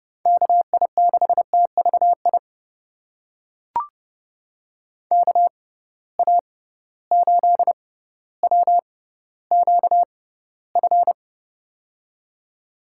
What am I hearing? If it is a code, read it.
Morse code, heard as a keyed tone, about 30 wpm, spaced as for 15 wpm, 700 Hz. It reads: KI6TVI KA8WQF
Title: USA Call Signs - 6 Characters - 6x Character Spacing - 30wpm